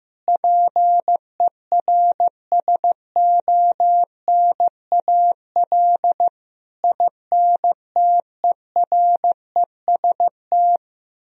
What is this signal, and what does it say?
Morse code, heard as a keyed tone, about 15 wpm, 700 Hz. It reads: PERSONAL INTEREST
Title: Sentences from Top 1000 Words - Review Part 4 - 15wpm